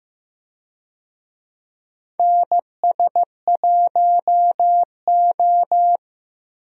Morse code, heard as a keyed tone, about 16 wpm, 700 Hz. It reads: NS1O
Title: USA Call Signs - 4 Characters - Repeated 3x - 15wpm